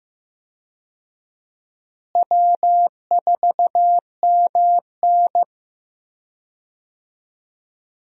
Morse code, heard as a keyed tone, about 15 wpm, 700 Hz. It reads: W4MN
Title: USA Call Signs - 4 Characters - Repeated 3x - 15wpm